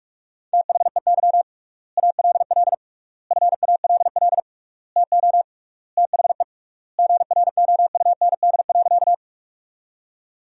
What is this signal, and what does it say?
Morse code, heard as a keyed tone, about 45 wpm, 700 Hz. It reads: THEY ALL FALL TO THE GROUND.
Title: 3x Word-Spacing - Sentences from Top 500 Words - 45wpm